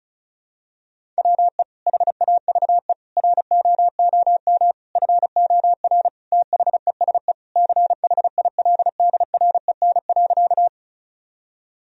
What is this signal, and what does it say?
Morse code, heard as a keyed tone, about 34 wpm, 700 Hz. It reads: WE HAVE ROOM FOR THESE CHILDREN.
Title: Sentences from Top 1000 Words - Review Part 1 - 35wpm